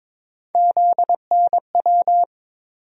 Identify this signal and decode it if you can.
Morse code, heard as a keyed tone, about 22 wpm, 700 Hz. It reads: ZNW
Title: USA Call Sign Postfixes - 3 Letters 22wpm